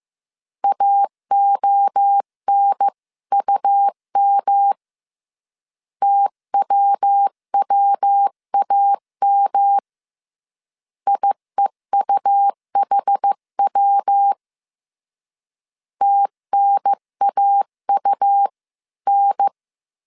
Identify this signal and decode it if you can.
Morse code, heard as a keyed tone, about 15 wpm, 800 Hz. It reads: AONUM TWWAM IEUHW TNAUN
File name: CW / DET / 1 3 1500C S A 14wpm 2447